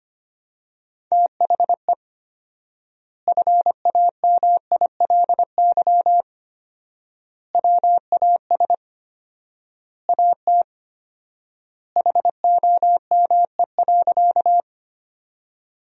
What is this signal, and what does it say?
Morse code, heard as a keyed tone, about 25 wpm, 700 Hz. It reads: THE FAMILY WAS AT HOME.
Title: 4x Word-Spacing - Sentences from Top 400 Words - 25wpm